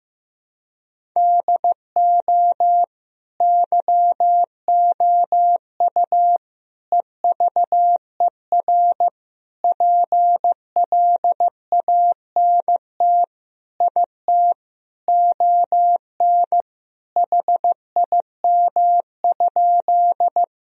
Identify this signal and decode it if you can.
Morse code, heard as a keyed tone, about 16 wpm, 700 Hz. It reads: DO YOU EVER PLANT IT ON HIM?
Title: Sentences from Top 1000 Words - Review Part 1 - 15wpm